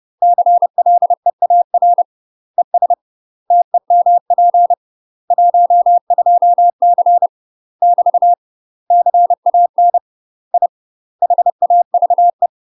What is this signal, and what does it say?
Morse code, heard as a keyed tone, about 30 wpm, 700 Hz. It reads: CLEAR ES TEMP 12C <BT> CAN I HAVE
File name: UK MASTER QSOs TRAINING / 30WPM / UK MASTER QSOs 0020